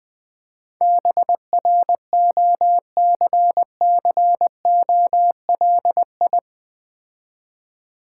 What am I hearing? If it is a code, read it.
Morse code, heard as a keyed tone, about 20 wpm, 700 Hz. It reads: BROCCOLI
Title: Difficult Words with Hints - 20wpm